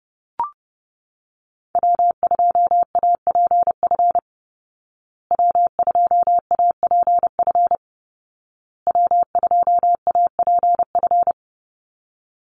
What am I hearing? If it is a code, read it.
Morse code, heard as a keyed tone, about 30 wpm, 700 Hz. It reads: W2APF W2APF W2APF
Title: Call Signs Repeated 2x - 30wpm